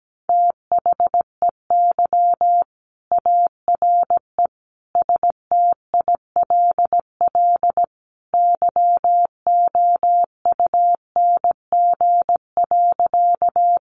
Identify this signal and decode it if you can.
Morse code, heard as a keyed tone, about 17 wpm, 700 Hz. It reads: THEY ARE STILL YOUNG.
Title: Sentences from Top 400 Words 17wpm